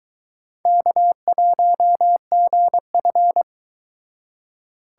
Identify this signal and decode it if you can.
Morse code, heard as a keyed tone, about 22 wpm, 700 Hz. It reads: K1GF